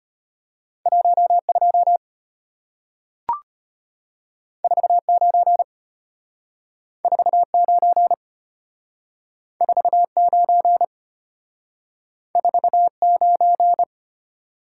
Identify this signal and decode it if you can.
Morse code, heard as a keyed tone, about 34 wpm, 700 Hz. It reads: 12 49 49 49 49